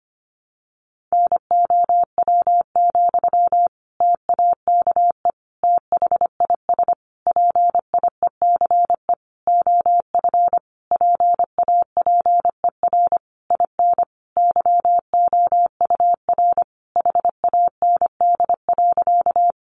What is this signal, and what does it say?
Morse code, heard as a keyed tone, about 25 wpm, 700 Hz. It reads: NOW, TAKE THIS PIECE OF PAPER IN YOUR HAND.
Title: Sentences from Top 400 Words 25wpm